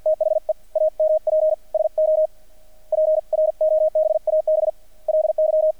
Morse code, heard as a keyed tone, about 50 wpm, spaced as for 40 wpm, 620 Hz. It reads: THE AMP SO PROBABLY
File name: archived / CRSnet